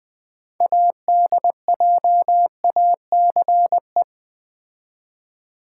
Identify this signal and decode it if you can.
Morse code, heard as a keyed tone, about 20 wpm, 700 Hz. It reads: ADJACE
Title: Build A Word - 3 Letters At A Time - 8-Letter Words - 20wpm